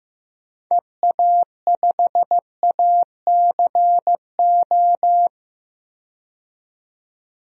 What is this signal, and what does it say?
Morse code, heard as a keyed tone, about 15 wpm, 700 Hz. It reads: EA5ACO
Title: DXCC-100 Call Signs - 6 Characters - 15wpm